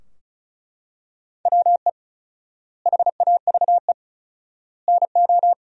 Morse code, heard as a keyed tone, about 36 wpm, 700 Hz. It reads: WE HAVE NO
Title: 4x Word-Spacing - Sentences from Top 100 Words - 35wpm